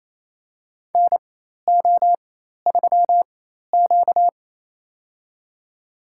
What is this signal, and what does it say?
Morse code, heard as a keyed tone, about 28 wpm, spaced as for 13 wpm, 700 Hz. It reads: NO3Q
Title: USA Call Signs - 4 Characters - 4x Character Spacing - 28wpm